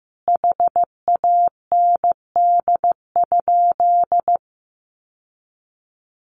Morse code, heard as a keyed tone, about 15 wpm, 700 Hz. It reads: HAND?
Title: Sentences from Top 1500 Words 15wpm